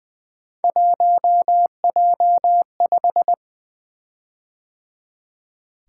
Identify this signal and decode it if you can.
Morse code, heard as a keyed tone, about 20 wpm, 700 Hz. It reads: 1J5
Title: Three Letter-Number Combinations - 20wpm